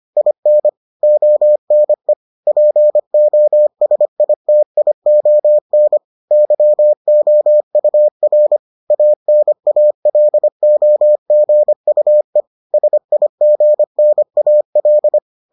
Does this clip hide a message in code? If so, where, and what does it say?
Yes: Morse code from 0.2 to 15.2 s, heard as a keyed tone, about 25 wpm, 600 Hz. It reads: IN ONE POSITION YOUR ANALOGUE SIGNAL